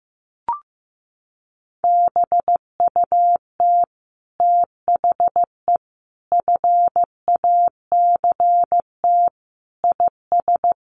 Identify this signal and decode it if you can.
Morse code, heard as a keyed tone, about 15 wpm, 700 Hz. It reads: BUT THE FACT IS
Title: Sets of 5 Words Encore2 15wpm